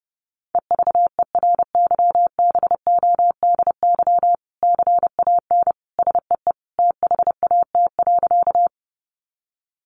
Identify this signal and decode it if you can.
Morse code, heard as a keyed tone, about 30 wpm, 700 Hz. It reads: EVERYBODY CAN SEE THAT.
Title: Sentences from Top 1400 Words 30wpm